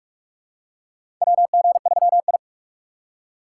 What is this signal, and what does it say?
Morse code, heard as a keyed tone, about 47 wpm, 700 Hz. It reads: WG3I
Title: USA Call Signs - 4 Characters - 4x Character Spacing - 45wpm